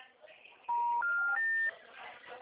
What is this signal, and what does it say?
Special information tone, with speech.